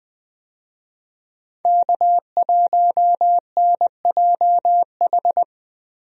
Morse code, heard as a keyed tone, about 19 wpm, 700 Hz. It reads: K1NJH
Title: USA Call Signs - 5 Characters - Repeated 3x - 20wpm